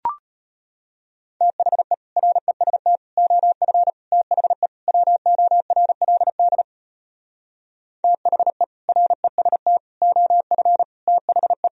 Morse code, heard as a keyed tone, about 33 wpm, 700 Hz. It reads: THE REST OF THE WORLD THE REST OF THE